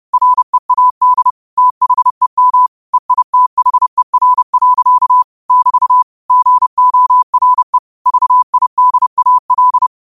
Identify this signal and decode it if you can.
Morse code, heard as a keyed tone, about 30 wpm, 1000 Hz. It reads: READ THEM EITHER. <BT> GORE VIDAL